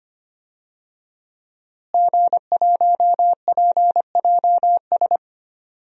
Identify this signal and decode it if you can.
Morse code, heard as a keyed tone, about 25 wpm, 700 Hz. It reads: G1PJS
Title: DXCC-100 Call Signs - 5 Characters - Repeated 3x - 25wpm